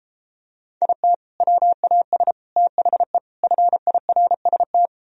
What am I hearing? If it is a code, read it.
Morse code, heard as a keyed tone, about 33 wpm, 700 Hz. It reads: IT WAS THE FIRST